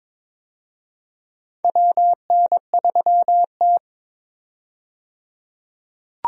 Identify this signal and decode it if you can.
Morse code, heard as a keyed tone, about 22 wpm, 700 Hz. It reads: WN3T
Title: USA Call Signs - 4 Characters - 22wpm